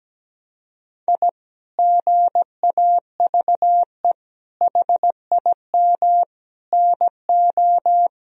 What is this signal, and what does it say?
Morse code, heard as a keyed tone, about 17 wpm, 700 Hz. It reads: I GAVE HIM NO
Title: Sentences from Top 1000 Words - Review Part 2 - 17wpm